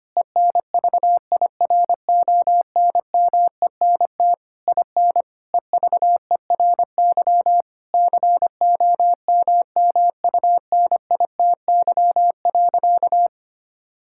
Morse code, heard as a keyed tone, about 25 wpm, 700 Hz. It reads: ENVIRONMENT IN EVERY COMMUNITY.
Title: Sentences from Top 1000 Words - Review Part 5 - 25wpm